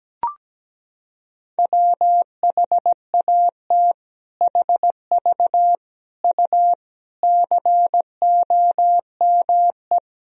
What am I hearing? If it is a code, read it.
Morse code, heard as a keyed tone, about 17 wpm, 700 Hz. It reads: WHAT HV U COME
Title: Sentences from Top 100 Words and Abbreviations 17wpm